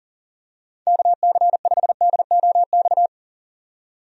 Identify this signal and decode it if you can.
Morse code, heard as a keyed tone, about 37 wpm, 700 Hz. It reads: KC5DOX
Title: USA Call Signs - 6 Characters - 40wpm